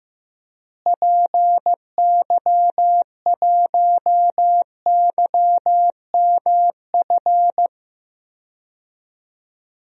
Morse code, heard as a keyed tone, about 15 wpm, 700 Hz. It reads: PY1YMF
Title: DXCC-100 Call Signs - 6 Characters - Repeated 2x - 15wpm